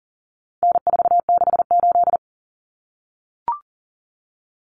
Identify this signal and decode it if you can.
Morse code, heard as a keyed tone, about 40 wpm, 700 Hz. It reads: N468
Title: Common Cut Numbers - 4-Digit 40wpm